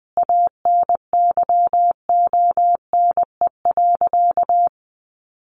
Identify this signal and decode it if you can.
Morse code, heard as a keyed tone, about 20 wpm, 700 Hz. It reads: ANYONE.
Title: Sentences from Top 900 Words 20wpm